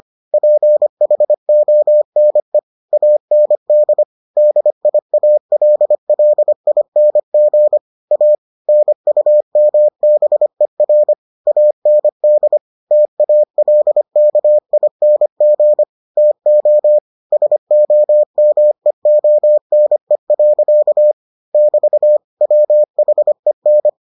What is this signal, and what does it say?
Morse code, heard as a keyed tone, about 25 wpm, 600 Hz. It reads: PHONE AND DIALLING A NUMBER AND TALKING TO SOMEONE. <BT> WHEN